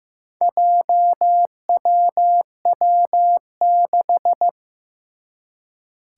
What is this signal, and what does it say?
Morse code, heard as a keyed tone, about 15 wpm, 700 Hz. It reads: JWW6
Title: Four Challenging Letter-Number Combinations 15wpm